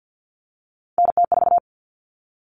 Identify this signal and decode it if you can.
Morse code, heard as a keyed tone, about 50 wpm, 700 Hz. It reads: NT4
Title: All Cut Numbers - 3-Digit 50wpm